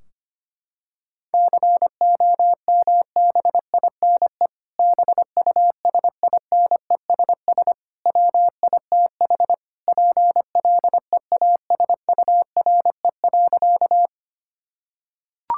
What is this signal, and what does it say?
Morse code, heard as a keyed tone, about 25 wpm, 700 Hz. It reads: COMBINE BUSINESS WITH PLEASURE.
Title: Sentences from Top 1600 Words 25wpm